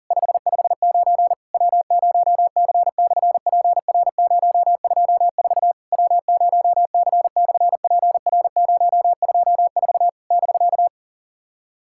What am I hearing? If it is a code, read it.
Morse code, heard as a keyed tone, about 40 wpm, 690 Hz. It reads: 559 W0C/PR024 W0C/PR024 <BK>
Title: SOTA QSO Practice 40wpm